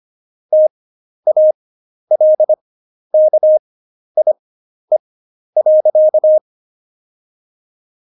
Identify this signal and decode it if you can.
Morse code, heard as a keyed tone, about 25 wpm, spaced as for 11 wpm, 620 Hz. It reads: TALKIE.
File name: Jokes-25-char-10-eff